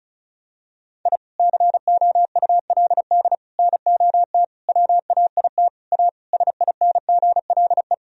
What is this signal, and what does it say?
Morse code, heard as a keyed tone, about 36 wpm, 700 Hz. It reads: I COULD NOT WAIT A SINGLE
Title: Sentences from Top 1000 Words - Review Part 4 - 35wpm